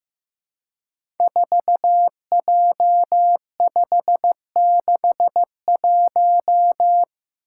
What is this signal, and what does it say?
Morse code, heard as a keyed tone, about 15 wpm, 700 Hz. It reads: 4J561
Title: Five Challenging Letter-Number Combinations 15wpm